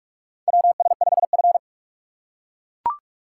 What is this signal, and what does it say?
Morse code, heard as a keyed tone, about 44 wpm, 690 Hz. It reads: WS5F
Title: USA Call Signs - 4 Characters - 3x Character Spacing - 45wpm